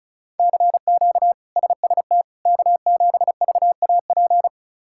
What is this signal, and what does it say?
Morse code, heard as a keyed tone, about 36 wpm, 700 Hz. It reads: CQ SST K7VAP
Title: K1USN SST QSO Practice - 35wpm